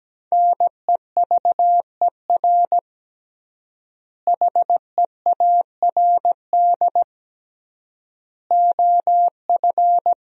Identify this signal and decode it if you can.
Morse code, heard as a keyed tone, about 17 wpm, 700 Hz. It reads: NEVER HEARD OF
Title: 3x Word-Spacing - Sentences from Top 400 Words - 17wpm